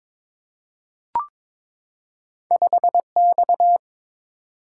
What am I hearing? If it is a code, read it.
Morse code, heard as a keyed tone, about 22 wpm, 690 Hz. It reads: 5X
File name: practice / 22wpm / Two Letter-Number Combinations - 22wpm